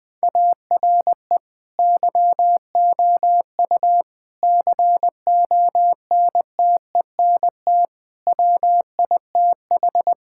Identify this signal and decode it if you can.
Morse code, heard as a keyed tone, about 20 wpm, 700 Hz. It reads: ARE YOU CONTENT WITH
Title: Sentences from Top 1000 Words - Review Part 6 - 20wpm